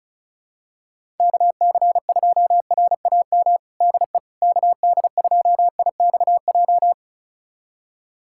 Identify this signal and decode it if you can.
Morse code, heard as a keyed tone, about 36 wpm, 700 Hz. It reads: KC2RAM DE KD2IXJ
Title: USA Callsign DE Callsign - 6 Characters - 35wpm